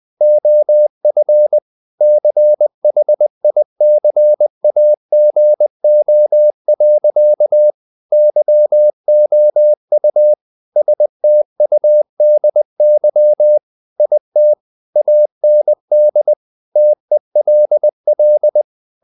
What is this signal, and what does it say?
Morse code, heard as a keyed tone, about 20 wpm, 600 Hz. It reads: OF CHICAGO. YOU STUDY IT AND TELL
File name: the old man and the sea / the old man and the sea part 1 20wpm